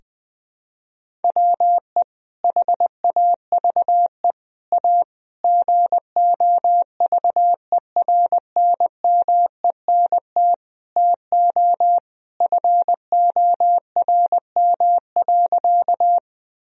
Morse code, heard as a keyed tone, about 20 wpm, 700 Hz. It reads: WE HAVE A GOVERNMENT TO FORM.
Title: Sentences from Top 1000 Words - Review Part 1 - 20wpm